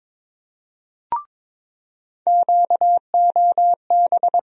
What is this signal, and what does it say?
Morse code, heard as a keyed tone, about 21 wpm, 700 Hz. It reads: QOB
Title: USA Call Sign Postfixes - 3 Letters 22wpm